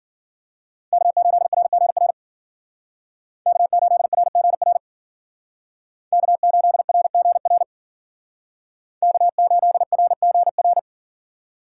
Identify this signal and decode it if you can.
Morse code, heard as a keyed tone, about 50 wpm, 700 Hz. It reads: K8RGR K8RGR K8RGR K8RGR